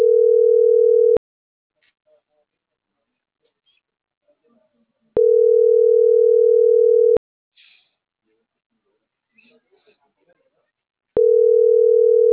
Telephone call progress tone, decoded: ringback tone

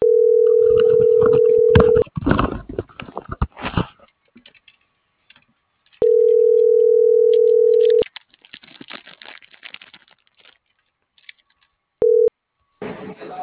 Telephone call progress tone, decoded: ringback tone

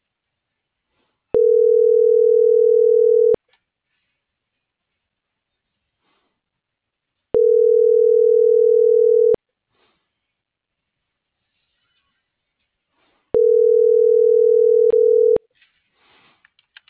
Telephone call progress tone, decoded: ringback tone

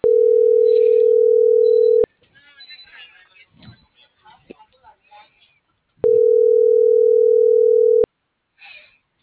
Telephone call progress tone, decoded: ringback tone

